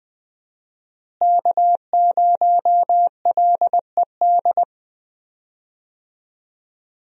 Morse code, heard as a keyed tone, about 20 wpm, 700 Hz. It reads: K0LED
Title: USA Call Signs - 5 Characters - Repeated 2x - 20wpm